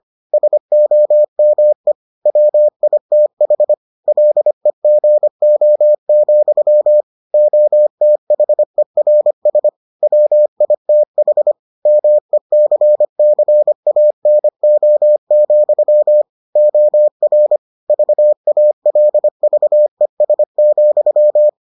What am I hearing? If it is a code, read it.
Morse code, heard as a keyed tone, about 25 wpm, 600 Hz. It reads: SOME WITH LEGO, OTHERS WITH MECCANO, OR VALVES,